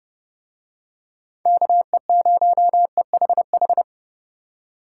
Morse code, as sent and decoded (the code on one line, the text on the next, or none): -.- . ----- . .... ....
KE0EHH